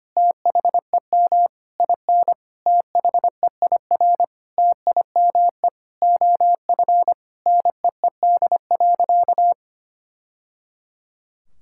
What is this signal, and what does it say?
Morse code, heard as a keyed tone, about 25 wpm, 700 Hz. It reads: THEM IN THEIR TIME OF NEED.
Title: Sentences from Top 1000 Words - Review Part 5 - 25wpm